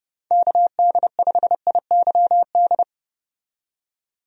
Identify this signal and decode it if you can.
Morse code, heard as a keyed tone, about 30 wpm, 700 Hz. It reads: KD5IYD
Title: USA Call Signs - 6 Characters - 30wpm